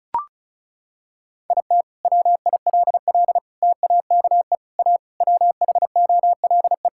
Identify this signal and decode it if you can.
Morse code, heard as a keyed tone, about 36 wpm, 700 Hz. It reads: IT WILL TAKE A WHOLE
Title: Sentences from Top 1000 Words - Review Part 5 - 35wpm